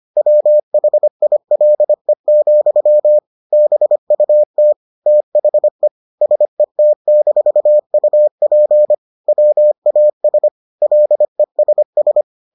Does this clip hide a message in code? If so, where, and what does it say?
Yes: Morse code from 0.2 to 12.2 s, heard as a keyed tone, about 25 wpm, 600 Hz. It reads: WHILE, BUT THE SET-UP WAS LESS